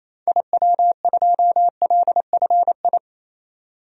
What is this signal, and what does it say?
Morse code, heard as a keyed tone, about 28 wpm, 700 Hz. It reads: IW2LFI
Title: DXCC-100 Call Signs - 6 Characters - 28wpm